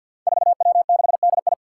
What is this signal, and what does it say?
Morse code, heard as a keyed tone, about 46 wpm, 700 Hz. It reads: 4W6DI
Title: Five Challenging Letter-Number Combinations 50wpm